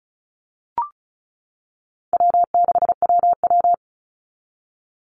Morse code, heard as a keyed tone, about 34 wpm, 700 Hz. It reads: W6WW
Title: NA SOTA Chaser Callsigns 35wpm